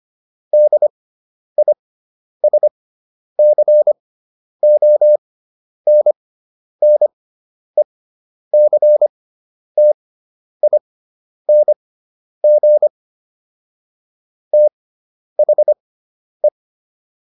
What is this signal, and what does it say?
Morse code, heard as a keyed tone, about 25 wpm, spaced as for 10 wpm, 600 Hz. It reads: DISCONNECTING THE